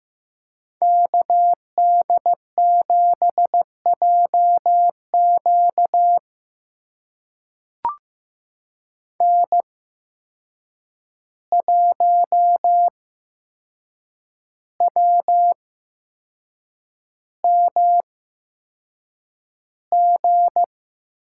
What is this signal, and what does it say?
Morse code, heard as a keyed tone, about 15 wpm, spaced as for 6 wpm, 700 Hz. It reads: KD7JQ N1WMG
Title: USA Call Signs - 5 Characters - 8x Character Spacing - 15wpm